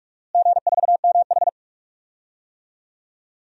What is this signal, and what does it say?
Morse code, heard as a keyed tone, about 44 wpm, 700 Hz. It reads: G4MH
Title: DXCC-100 Call Signs - 4 Characters - Repeated 2x - 45wpm